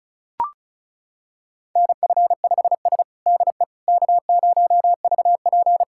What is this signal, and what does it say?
Morse code, heard as a keyed tone, about 36 wpm, 700 Hz. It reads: NF5S DE K0VP
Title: USA Callsign DE Callsign - 4 Characters - 35wpm